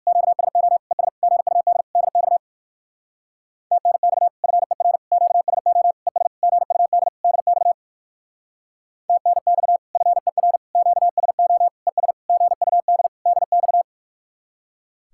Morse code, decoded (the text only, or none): QSO ES GUD DX TNX FER QSO ES GUD DX TNX FER QSO ES GUD DX